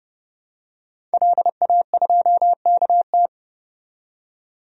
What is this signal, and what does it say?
Morse code, heard as a keyed tone, about 30 wpm, 700 Hz. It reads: LA2KT
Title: DXCC-100 Call Signs - 5 Characters - 30wpm